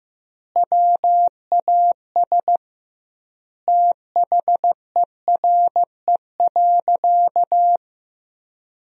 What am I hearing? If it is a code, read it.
Morse code, heard as a keyed tone, about 16 wpm, 700 Hz. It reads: WAS THERE.
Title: 2x Word-Spacing - Sentences from Top 500 Words - 15wpm